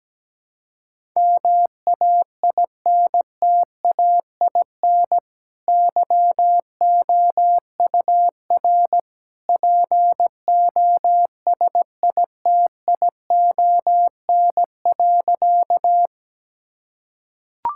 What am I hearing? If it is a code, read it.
Morse code, heard as a keyed tone, about 17 wpm, 700 Hz. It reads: MAINTAIN YOUR POSITION.
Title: Sentences from Top 1000 Words - Review Part 5 - 17wpm